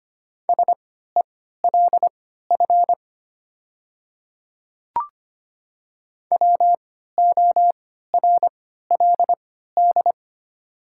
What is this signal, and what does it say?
Morse code, heard as a keyed tone, about 25 wpm, spaced as for 14 wpm, 700 Hz. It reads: SELF WORLD
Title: 3x Character Spacing - Top 200 Words - 25wpm